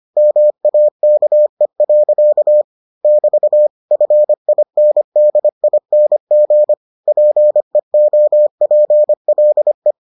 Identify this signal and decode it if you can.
Morse code, heard as a keyed tone, about 25 wpm, 600 Hz. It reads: MAKE. <BT> FINDING PEOPLE